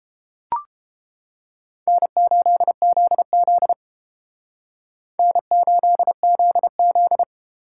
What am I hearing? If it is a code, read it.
Morse code, heard as a keyed tone, about 29 wpm, 700 Hz. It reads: N8ZZ N8ZZ